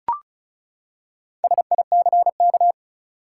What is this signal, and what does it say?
Morse code, heard as a keyed tone, about 34 wpm, 700 Hz. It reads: SICK
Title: Four Letter Words - 35wpm